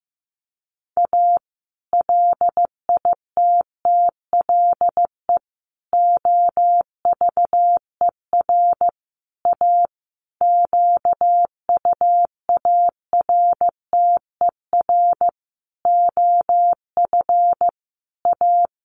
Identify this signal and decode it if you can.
Morse code, heard as a keyed tone, about 15 wpm, 700 Hz. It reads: A LITTLE OVER A QUARTER OF A
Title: Sentences from Top 1300 Words 15wpm